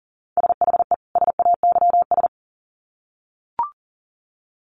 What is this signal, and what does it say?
Morse code, heard as a keyed tone, about 40 wpm, 700 Hz. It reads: SHE SAYS
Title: Sets of 2 Words 40wpm